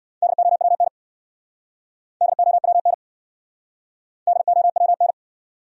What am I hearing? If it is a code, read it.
Morse code, heard as a keyed tone, about 67 wpm, 700 Hz. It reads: DOWN DOWN DOWN